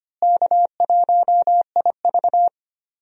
Morse code, heard as a keyed tone, about 25 wpm, 700 Hz. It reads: K1IV
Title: USA Call Signs - 4 Characters - Repeated 3x - 25wpm